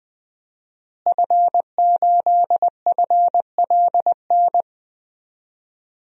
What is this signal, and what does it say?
Morse code, heard as a keyed tone, about 20 wpm, 700 Hz. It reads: F8FLN
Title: DXCC-100 Call Signs - 5 Characters - Repeated 2x - 20wpm